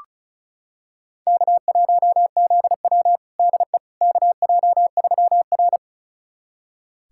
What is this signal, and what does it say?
Morse code, heard as a keyed tone, about 34 wpm, 700 Hz. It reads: K1ZW DE KJ3R
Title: USA Callsign DE Callsign - 4 Characters - 35wpm